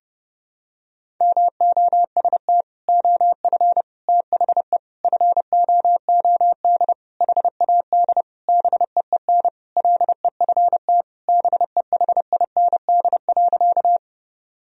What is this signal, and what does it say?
Morse code, heard as a keyed tone, about 32 wpm, 700 Hz. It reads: MOST OF THE FOOD HAD BEEN LEFT BEHIND.
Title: Sentences from Top 1000 Words - Review Part 1 - 30wpm